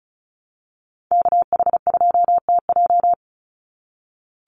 Morse code, heard as a keyed tone, about 34 wpm, 700 Hz. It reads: KH2TJ
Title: NA SOTA Activator Callsigns 35wpm